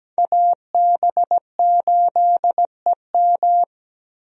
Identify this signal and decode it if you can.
Morse code, heard as a keyed tone, about 17 wpm, 700 Hz. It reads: AB8EM
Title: USA Call Signs - 5 Characters - Repeated 2x - 17wpm